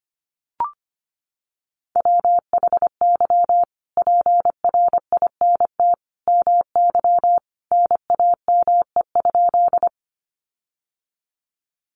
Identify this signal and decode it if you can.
Morse code, heard as a keyed tone, about 25 wpm, 700 Hz. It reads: WHY PRINT MY NAME?
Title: Sentences from Top 1400 Words 25wpm